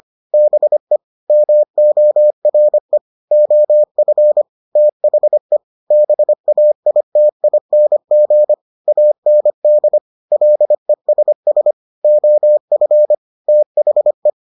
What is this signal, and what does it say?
Morse code, heard as a keyed tone, about 25 wpm, 600 Hz. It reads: BE MORE OF THE BAITING AND LESS OF THE